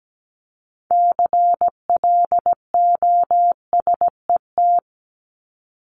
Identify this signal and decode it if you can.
Morse code, heard as a keyed tone, about 17 wpm, 700 Hz. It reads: CLOSET
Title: Top 1751-2000 Words 17wpm